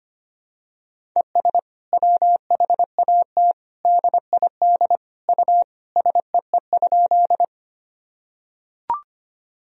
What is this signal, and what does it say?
Morse code, heard as a keyed tone, about 25 wpm, 700 Hz. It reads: ES WHAT DID U SEE?
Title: Sentences from Top 100 Words and Abbreviations 25wpm